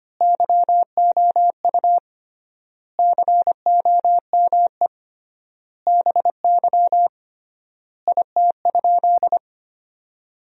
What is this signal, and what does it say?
Morse code, heard as a keyed tone, about 25 wpm, 700 Hz. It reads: YOU COME BY IT?
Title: 3x Word-Spacing - Sentences from Top 100 Words - 25wpm